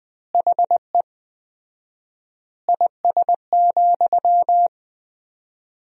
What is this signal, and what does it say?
Morse code, heard as a keyed tone, about 21 wpm, 700 Hz. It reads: HE IS,
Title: 4x Word-Spacing - Sentences from Top 100 Words - 20wpm